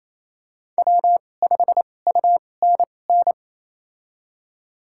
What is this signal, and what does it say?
Morse code, heard as a keyed tone, about 28 wpm, spaced as for 20 wpm, 700 Hz. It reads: W5UNN
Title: USA Call Signs - 5 Characters - 2x Character Spacing - 28wpm